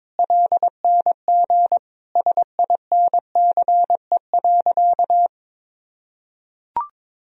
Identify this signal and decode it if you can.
Morse code, heard as a keyed tone, about 22 wpm, 700 Hz. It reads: LNG SINCE.
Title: Sentences from Top 300 Words and Abbreviations 22wpm